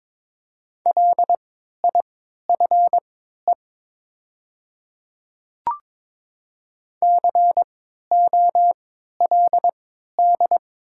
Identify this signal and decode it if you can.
Morse code, heard as a keyed tone, about 22 wpm, spaced as for 13 wpm, 700 Hz. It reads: LIFE COLD
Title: 3x Character Spacing - Top 500 Words - 22wpm